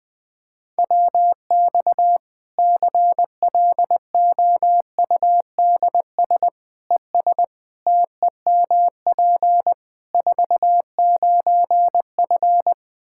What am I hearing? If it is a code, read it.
Morse code, heard as a keyed tone, about 20 wpm, 700 Hz. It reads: WX CLOUDS ES TEMP 49F
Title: K1USN SST QSO Practice 20wpm